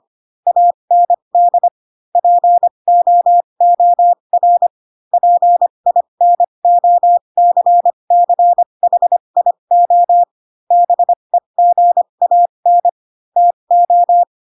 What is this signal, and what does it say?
Morse code, heard as a keyed tone, about 25 wpm, spaced as for 22 wpm, 700 Hz. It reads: AND POOR PINOCCHIO BEGAN TO
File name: Pinocchio / Pinocchio ch7 18 25 ik1hge